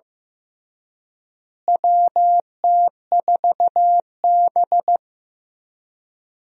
Morse code, heard as a keyed tone, about 15 wpm, 700 Hz. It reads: WT4B
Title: USA Call Signs - 4 Characters - Repeated 3x - 15wpm